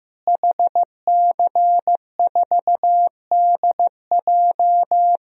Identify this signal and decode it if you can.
Morse code, heard as a keyed tone, about 15 wpm, 700 Hz. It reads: HC4DJ
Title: DXCC-100 Call Signs - 5 Characters - Repeated 3x - 15wpm